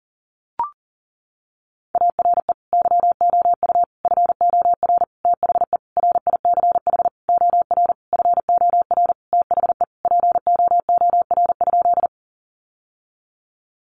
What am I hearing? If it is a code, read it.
Morse code, heard as a keyed tone, about 40 wpm, 700 Hz. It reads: ARE YOU FOR THE RICH OR FOR THE POOR?
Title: Sentences from Top 1100 Words 40wpm